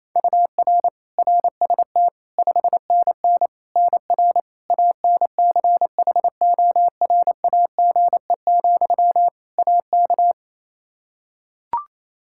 Morse code, heard as a keyed tone, about 28 wpm, 700 Hz. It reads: UR RST 5NN NR ANCHORAGE, AK
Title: Practice QSO - Exchange 1 - Beginning - 28wpm